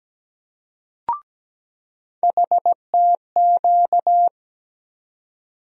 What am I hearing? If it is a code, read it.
Morse code, heard as a keyed tone, about 17 wpm, 700 Hz. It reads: HTQ
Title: USA Call Sign Postfixes - 3 Letters 17wpm